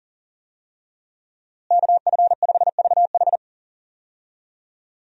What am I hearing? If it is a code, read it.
Morse code, heard as a keyed tone, about 40 wpm, 700 Hz. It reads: KF5VH
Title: USA Call Signs - 5 Characters - Repeated 3x - 40wpm